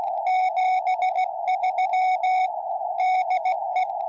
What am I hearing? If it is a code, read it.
Morse code, heard as a keyed tone, about 17 wpm, 740 Hz, with radio noise. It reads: 73 DE
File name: tropo / gm4wja